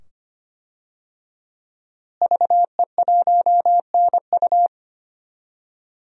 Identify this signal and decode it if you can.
Morse code, heard as a keyed tone, about 25 wpm, 700 Hz. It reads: VE1NU
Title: DXCC-100 Call Signs - 5 Characters - Repeated 2x - 25wpm